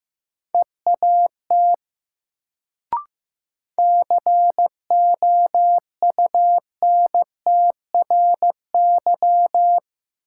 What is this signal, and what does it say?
Morse code, heard as a keyed tone, about 15 wpm, 700 Hz. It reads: EAT COUNTRY
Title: Sentences from Top 300 Words - New Word Drill - 15wpm